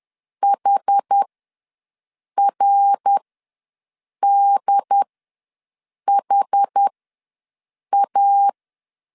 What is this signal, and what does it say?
Morse code, heard as a keyed tone, about 10 wpm, spaced as for 6 wpm, 800 Hz. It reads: HRDHA
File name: LEZIONI / 5 LBVC / 5 LBVC ws 10wpm 3733